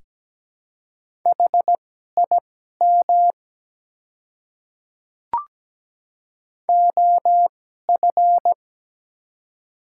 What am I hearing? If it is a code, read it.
Morse code, heard as a keyed tone, about 17 wpm, spaced as for 12 wpm, 700 Hz. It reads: HIM OF
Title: 2x Character Spacing - Top 100 Words - 17wpm